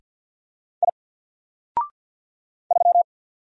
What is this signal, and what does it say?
Morse code, heard as a keyed tone, about 48 wpm, 700 Hz. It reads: I 3
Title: Single Character (Letter:Number:Punctuation:Prosign) 50wpm